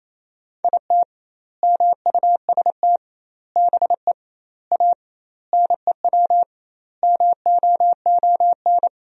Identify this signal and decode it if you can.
Morse code, heard as a keyed tone, about 28 wpm, 700 Hz. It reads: IT MUST BE A NEW MOON
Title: 2x Word-Spacing - Sentences from Top 500 Words - 28wpm